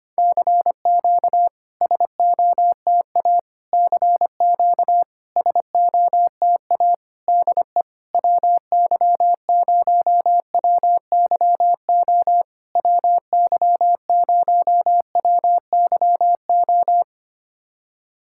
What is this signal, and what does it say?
Morse code, heard as a keyed tone, about 25 wpm, 700 Hz. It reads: CQ SOTA CQ SOTA DE WY0WYO WY0WYO
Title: SOTA QSO Practice 25wpm